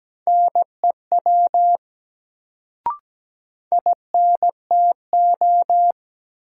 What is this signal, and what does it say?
Morse code, heard as a keyed tone, about 17 wpm, 700 Hz. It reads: NEW INTO
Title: Sets of 2 Words - New Word Drill - 17wpm